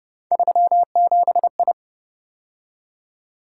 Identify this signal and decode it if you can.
Morse code, heard as a keyed tone, about 28 wpm, 700 Hz. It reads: 37I